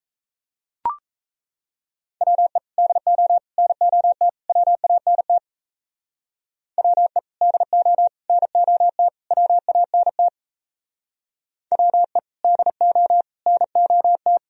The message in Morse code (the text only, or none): WE DO NOT WANT WE DO NOT WANT WE DO NOT